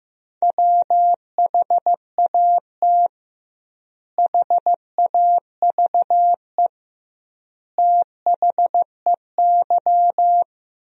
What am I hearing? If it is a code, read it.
Morse code, heard as a keyed tone, about 16 wpm, 700 Hz. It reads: WHAT HAVE THEY
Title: 2x Word-Spacing - Sentences from Top 500 Words - 15wpm